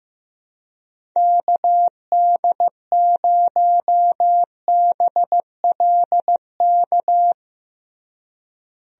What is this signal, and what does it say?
Morse code, heard as a keyed tone, about 16 wpm, 700 Hz. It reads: KD0BLK
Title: USA Call Signs - 6 Characters - Repeated 3x - 15wpm